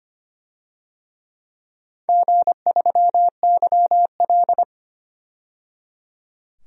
Morse code, heard as a keyed tone, about 25 wpm, 700 Hz. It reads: G3YL
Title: DXCC-100 Call Signs - 4 Characters - 25wpm